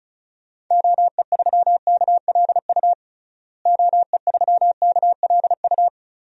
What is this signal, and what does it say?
Morse code, heard as a keyed tone, about 36 wpm, 700 Hz. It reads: OE3KLU OE3KLU
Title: DXCC-100 Call Signs - 6 Characters - Repeated 3x - 35wpm